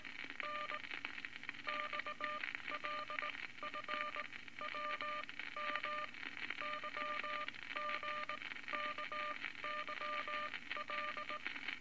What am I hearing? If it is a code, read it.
Morse code, heard as a keyed tone, about 19 wpm, spaced as for 15 wpm, 1200 Hz, with noise. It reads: N XLF WM YGKYL
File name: Morse Code Records / 19 B LCM 167 Twelve Words Per Minute